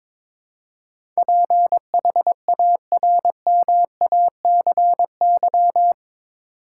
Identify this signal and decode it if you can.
Morse code, heard as a keyed tone, about 22 wpm, 700 Hz. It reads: PHARMACY
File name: practice / 22wpm / Eight Letter Words - 22wpm